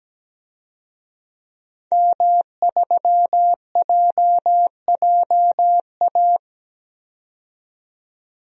Morse code, heard as a keyed tone, about 17 wpm, 700 Hz. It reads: M3JJA
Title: DXCC-100 Call Signs - 5 Characters - 17wpm